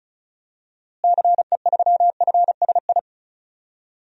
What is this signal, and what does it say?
Morse code, heard as a keyed tone, about 36 wpm, 700 Hz. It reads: CE3FSI
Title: DXCC-100 Call Signs - 6 Characters - Repeated 2x - 35wpm